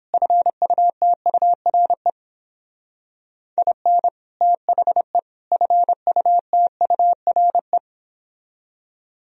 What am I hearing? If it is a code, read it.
Morse code, heard as a keyed tone, about 31 wpm, 700 Hz. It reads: FUTURE IN THE FUTURE